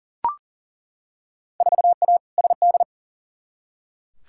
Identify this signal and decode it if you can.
Morse code, heard as a keyed tone, about 40 wpm, 700 Hz. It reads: 4A SD
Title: Field Day Exchanges - 40wpm